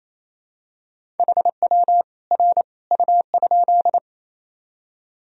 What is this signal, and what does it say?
Morse code, heard as a keyed tone, about 28 wpm, 700 Hz. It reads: HW R U?
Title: Sentences from Top 100 Words and Abbreviations 28wpm